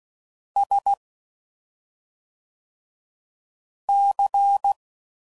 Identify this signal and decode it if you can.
Morse code, heard as a keyed tone, about 16 wpm, 800 Hz. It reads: S C
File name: Suffixes / Special ING's / SlowToFaster ING's